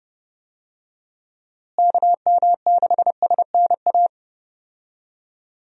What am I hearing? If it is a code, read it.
Morse code, heard as a keyed tone, about 30 wpm, 700 Hz. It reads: KM6SNA
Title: USA Call Signs - 6 Characters - Repeated 3x - 30wpm